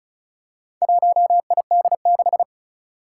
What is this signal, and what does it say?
Morse code, heard as a keyed tone, about 36 wpm, 700 Hz. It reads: 1ID6
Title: Four Challenging Letter-Number Combinations 35wpm